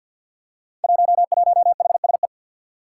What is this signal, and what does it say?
Morse code, heard as a keyed tone, about 53 wpm, 700 Hz. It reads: 11HSE
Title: Five Challenging Letter-Number Combinations 50wpm